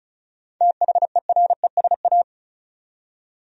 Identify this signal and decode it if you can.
Morse code, heard as a keyed tone, about 36 wpm, 700 Hz. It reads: THERESA
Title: USA Common Names - 35wpm